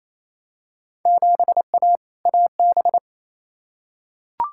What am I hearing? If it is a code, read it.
Morse code, heard as a keyed tone, about 28 wpm, 700 Hz. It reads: 7A AB
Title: Field Day Exchanges Encore - 28wpm